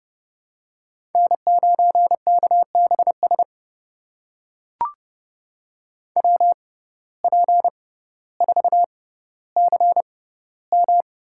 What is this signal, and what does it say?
Morse code, heard as a keyed tone, about 30 wpm, spaced as for 15 wpm, 700 Hz. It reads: N9KBS WP4CM
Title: USA Call Signs - 5 Characters - 6x Character Spacing - 30wpm